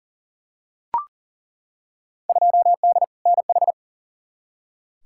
Morse code, heard as a keyed tone, about 40 wpm, 700 Hz. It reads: 2D NH
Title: Field Day Exchanges Encore - 40wpm